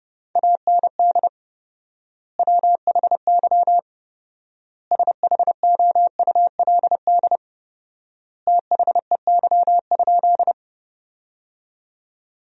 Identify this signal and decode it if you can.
Morse code, heard as a keyed tone, about 30 wpm, 700 Hz. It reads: AND WHY SHOULD THEY?
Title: 4x Word-Spacing - Sentences from Top 300 Words - 30wpm